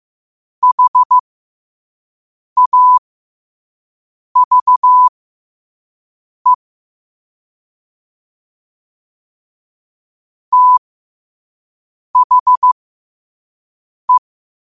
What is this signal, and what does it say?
Morse code, heard as a keyed tone, about 15 wpm, spaced as for 5 wpm, 1000 Hz. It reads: HAVE THE